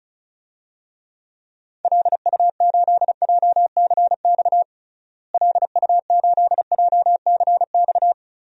Morse code, heard as a keyed tone, about 36 wpm, 700 Hz. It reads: LU8JCX LU8JCX
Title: DXCC-100 Call Signs - 6 Characters - Repeated 3x - 35wpm